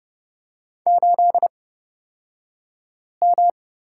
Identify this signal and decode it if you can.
Morse code, heard as a keyed tone, about 29 wpm, spaced as for 5 wpm, 700 Hz. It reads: 8M
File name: Lesson 38d - Letter q - Two Characters 30wpm